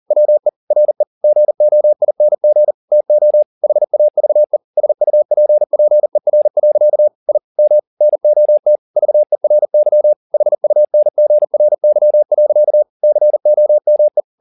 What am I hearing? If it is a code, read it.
Morse code, heard as a keyed tone, about 40 wpm, 600 Hz. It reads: WE RE GOING TO HAVE SUPPER. I M NOT VERY HUNGRY. COME